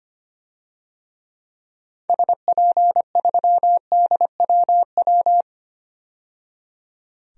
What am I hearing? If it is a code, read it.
Morse code, heard as a keyed tone, about 25 wpm, 700 Hz. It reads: SP3DWW
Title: DXCC-100 Call Signs - 6 Characters - 25wpm